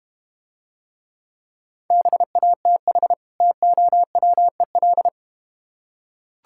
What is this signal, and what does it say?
Morse code, heard as a keyed tone, about 32 wpm, 700 Hz. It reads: BATH TOWEL